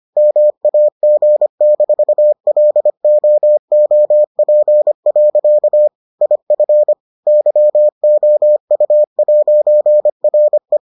Morse code, heard as a keyed tone, about 25 wpm, 600 Hz. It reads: MAG-LOOP. IF YOU'RE